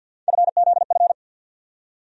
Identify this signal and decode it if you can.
Morse code, heard as a keyed tone, about 51 wpm, 690 Hz. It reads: FZF
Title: USA Call Sign Postfixes - 3 Letters 50wpm